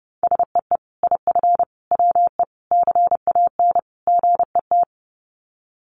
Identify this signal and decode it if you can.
Morse code, heard as a keyed tone, about 30 wpm, 700 Hz. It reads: SEE IF WE CAN GET
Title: Sets of 5 Words Encore2 30wpm